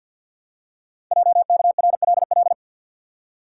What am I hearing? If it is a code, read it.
Morse code, heard as a keyed tone, about 50 wpm, 700 Hz. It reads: JKRLL
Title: Code Groups - 5 Letters - 50wpm